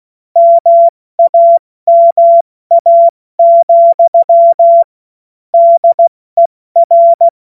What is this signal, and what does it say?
Morse code, heard as a keyed tone, about 16 wpm, spaced as for 14 wpm, 680 Hz. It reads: MAMA, DER
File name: Morsekurs / Witze / Witz 16. November 2021, 70 F80